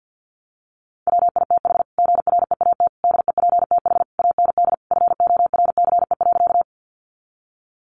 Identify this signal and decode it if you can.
Morse code, heard as a keyed tone, about 50 wpm, 700 Hz. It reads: WITH GREAT DEPTH AND FORCE.
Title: Sentences from Top 1200 Words 50wpm